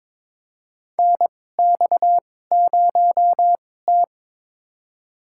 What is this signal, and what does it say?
Morse code, heard as a keyed tone, about 21 wpm, spaced as for 16 wpm, 700 Hz. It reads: NX0T
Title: USA Call Signs - 4 Characters - 2x Character Spacing - 22wpm